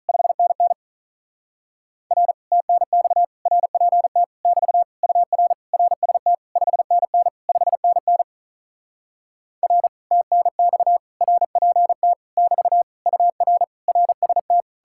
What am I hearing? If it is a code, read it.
Morse code, heard as a keyed tone, about 47 wpm, 700 Hz. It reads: 5NN R TNX RPT <BT> UR RST 5NN 5NN R TNX RPT <BT> UR RST